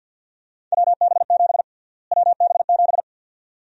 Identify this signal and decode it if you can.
Morse code, heard as a keyed tone, about 50 wpm, 700 Hz. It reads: WB7 WB7
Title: USA Call Sign Prefixes - Repeated 2x - 50wpm